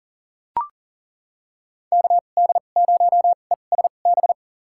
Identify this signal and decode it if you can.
Morse code, heard as a keyed tone, about 40 wpm, spaced as for 29 wpm, 700 Hz. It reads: KD0ESB
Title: USA Call Signs - 2x Character Spacing - 40wpm